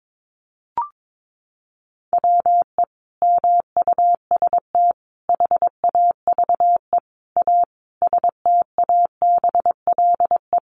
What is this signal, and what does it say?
Morse code, heard as a keyed tone, about 22 wpm, 700 Hz. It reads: WE MUST HAVE A STABLE
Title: Sentences from Top 1500 Words 22wpm